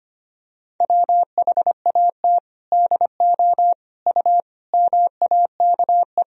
Morse code, heard as a keyed tone, about 25 wpm, 700 Hz. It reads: WHAT DO U MAKE
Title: Sentences from Top 100 Words and Abbreviations 25wpm